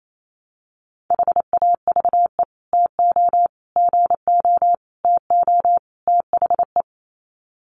Morse code, heard as a keyed tone, about 28 wpm, 700 Hz. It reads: HAVE TO GO TO THE
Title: Sets of 5 Words Encore 28wpm